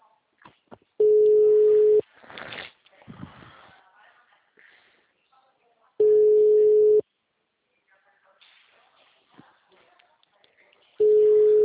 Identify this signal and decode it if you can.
Ringback tone (Europe, 425 Hz), with speech.